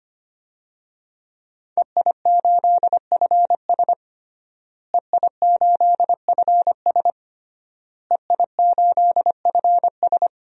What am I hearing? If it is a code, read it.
Morse code, heard as a keyed tone, about 25 wpm, 700 Hz. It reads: EI8FS EI8FS EI8FS
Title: DXCC-100 Call Signs - 5 Characters - Repeated 3x - 25wpm